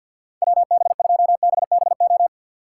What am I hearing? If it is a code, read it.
Morse code, heard as a keyed tone, about 49 wpm, 700 Hz. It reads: WB2BBO
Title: Speed Racing - USA 2x3 Call Signs 50wpm